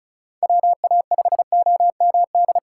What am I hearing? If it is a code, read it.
Morse code, heard as a keyed tone, about 36 wpm, 700 Hz. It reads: WA5OMD
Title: USA Call Signs - 6 Characters - 4x Character Spacing - 35wpm